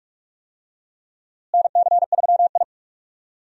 Morse code, heard as a keyed tone, about 46 wpm, 700 Hz. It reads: NC3I
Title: USA Call Signs - 4 Characters - Repeated 2x - 45wpm